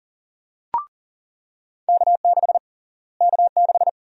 Morse code, heard as a keyed tone, about 40 wpm, 700 Hz. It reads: K6 K6
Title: USA Call Sign Prefixes - Repeated 2x - 40wpm